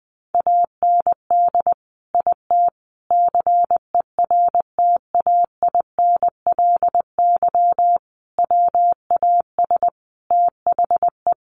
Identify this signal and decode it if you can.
Morse code, heard as a keyed tone, about 20 wpm, 700 Hz. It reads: AND IT CERTAINLY WAS THE
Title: Sentences from Top 1100 Words 20wpm